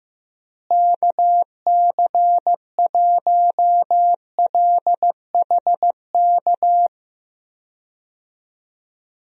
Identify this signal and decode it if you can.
Morse code, heard as a keyed tone, about 15 wpm, 700 Hz. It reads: KC1LHK
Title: USA Call Signs - 6 Characters - Repeated 3x - 15wpm